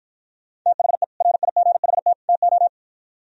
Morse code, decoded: THE RIGHT TO

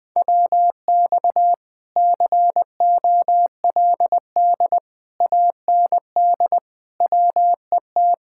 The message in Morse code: WX COLD AND WET

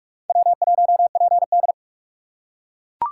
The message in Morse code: W1PD

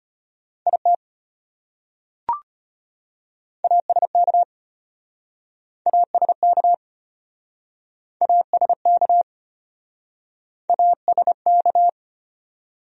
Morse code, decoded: IT ASK ASK ASK ASK

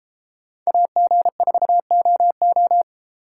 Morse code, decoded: AG4OO

